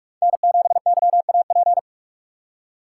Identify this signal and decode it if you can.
Morse code, heard as a keyed tone, about 42 wpm, 700 Hz. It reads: N 7 Y A P
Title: USA Call Signs - 5 Characters - 6x Character Spacing - 45wpm